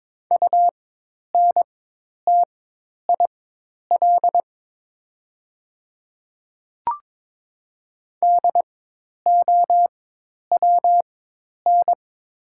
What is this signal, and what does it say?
Morse code, heard as a keyed tone, about 22 wpm, spaced as for 10 wpm, 700 Hz. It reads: UNTIL DOWN
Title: 4x Character Spacing - Top 500 Words - 22wpm